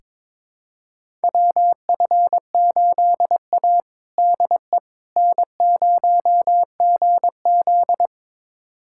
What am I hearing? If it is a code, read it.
Morse code, heard as a keyed tone, about 22 wpm, 700 Hz. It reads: WF8A DE N0GZ
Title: USA Callsign DE Callsign - 4 Characters - 22wpm